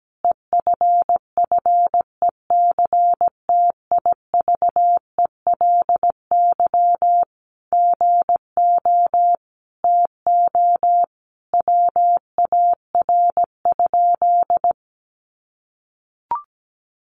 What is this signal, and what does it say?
Morse code, heard as a keyed tone, about 17 wpm, 700 Hz. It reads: EFFECTIVELY GO TO WAR?
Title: Sentences from Top 1400 Words 17wpm